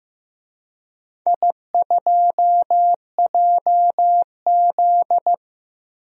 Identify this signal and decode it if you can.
Morse code, heard as a keyed tone, about 16 wpm, 700 Hz. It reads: I2JZ
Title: DXCC-100 Call Signs - 4 Characters - Repeated 3x - 15wpm